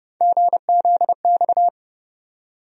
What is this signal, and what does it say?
Morse code, heard as a keyed tone, about 29 wpm, 700 Hz. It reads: GZX
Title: USA Call Sign Postfixes - 3 Letters 30wpm